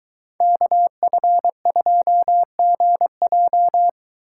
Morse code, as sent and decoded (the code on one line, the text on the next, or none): -.- ..-. ..--- --. .---
KF2GJ